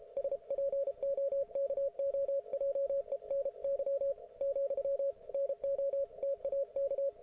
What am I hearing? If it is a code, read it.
Morse code, heard as a keyed tone, about 33 wpm, 550 Hz. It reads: SPOKOJENY , NOTAK
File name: Archiv / CUC157 / ok1fhi cul157